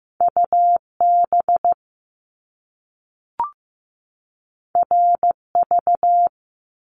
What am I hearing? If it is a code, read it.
Morse code, heard as a keyed tone, about 15 wpm, 700 Hz. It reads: UB RV
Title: Top Two Letter Combinations in Words 15wpm